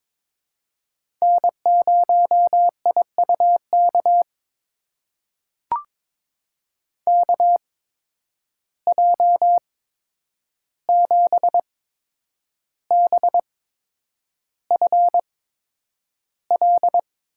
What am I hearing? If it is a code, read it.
Morse code, heard as a keyed tone, about 22 wpm, spaced as for 9 wpm, 700 Hz. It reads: N0IUK KJ7BFL
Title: USA Call Signs - 8x Character Spacing - 22wpm